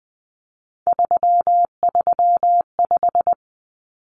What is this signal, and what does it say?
Morse code, heard as a keyed tone, about 20 wpm, 700 Hz. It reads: 335
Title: Numbers - 3-Digit 20wpm